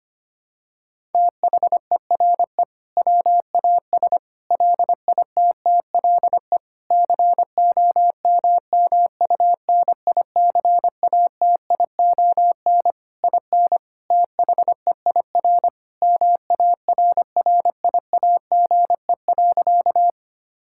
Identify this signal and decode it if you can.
Morse code, heard as a keyed tone, about 25 wpm, 700 Hz. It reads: THERE WAS LITTLE COMMUNICATION IN THEIR MARRIAGE.
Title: Sentences from Top 1000 Words - Review Part 6 - 25wpm